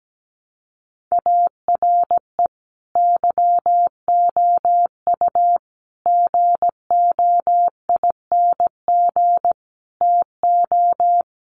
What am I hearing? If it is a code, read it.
Morse code, heard as a keyed tone, about 17 wpm, 700 Hz. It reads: ARE YOU GOING TO
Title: Sets of 5 Words 17wpm